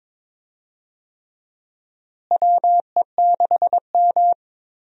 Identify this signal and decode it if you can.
Morse code, heard as a keyed tone, about 22 wpm, 700 Hz. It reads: WE6M
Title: USA Call Signs - 4 Characters - Repeated 3x - 22wpm